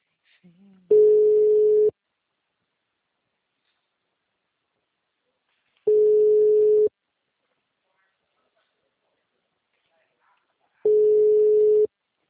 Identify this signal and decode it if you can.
Ringback tone (Europe, 425 Hz).